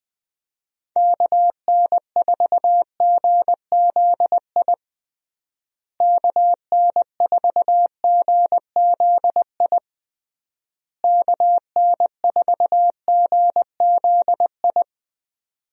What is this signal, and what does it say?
Morse code, heard as a keyed tone, about 20 wpm, 700 Hz. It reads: KN4GZI KN4GZI KN4GZI
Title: USA Call Signs - 6 Characters - Repeated 3x - 20wpm